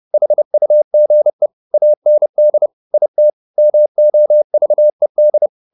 Morse code, heard as a keyed tone, about 30 wpm, 600 Hz. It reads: HUGE AND IT MOVED